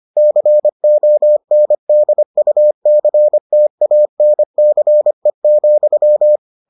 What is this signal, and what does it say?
Morse code, heard as a keyed tone, about 25 wpm, 600 Hz. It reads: CONDUCTANCE,